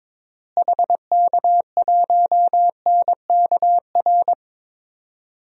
Morse code, heard as a keyed tone, about 22 wpm, 700 Hz. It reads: HK1NKR
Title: DXCC-100 Call Signs - 6 Characters - Repeated 2x - 22wpm